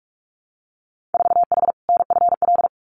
Morse code, heard as a keyed tone, about 46 wpm, 700 Hz. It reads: 4H NFL
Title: Winter Field Day Exchanges Encore 45wpm